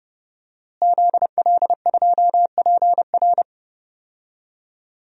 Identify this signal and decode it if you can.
Morse code, heard as a keyed tone, about 31 wpm, 700 Hz. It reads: ZL2PR
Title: DXCC-100 Call Signs - 5 Characters - Repeated 3x - 30wpm